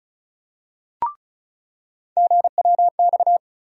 Morse code, heard as a keyed tone, about 34 wpm, 700 Hz. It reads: GWX
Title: USA Call Sign Postfixes - 3 Letters 35wpm